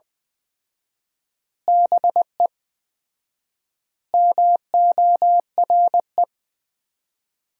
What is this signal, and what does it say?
Morse code, heard as a keyed tone, about 20 wpm, 700 Hz. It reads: BE MORE
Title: 4x Word-Spacing - Sentences from Top 500 Words - 20wpm